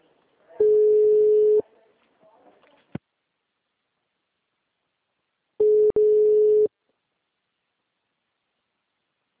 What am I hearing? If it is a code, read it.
Ringback tone (Europe, 425 Hz).